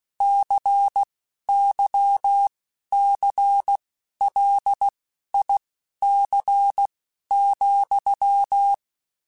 Morse code, heard as a keyed tone, about 16 wpm, spaced as for 11 wpm, 800 Hz. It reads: CYCLIC,
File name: Technical Subjects / Technical Subjects-mp3s / XL @10